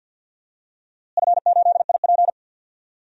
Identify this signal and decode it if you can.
Morse code, heard as a keyed tone, about 48 wpm, 700 Hz. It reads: F8IP
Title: DXCC-100 Call Signs - 4 Characters - Repeated 2x - 50wpm